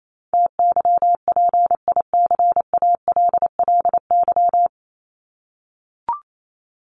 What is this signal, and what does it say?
Morse code, heard as a keyed tone, about 28 wpm, 700 Hz. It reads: TYPICALLY
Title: LY Suffix - Top 100 Words with Base Words 28wpm